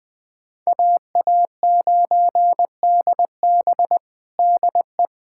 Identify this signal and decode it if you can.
Morse code, heard as a keyed tone, about 21 wpm, 700 Hz. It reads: AA9DB DE
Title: USA Callsign DE Callsign 20wpm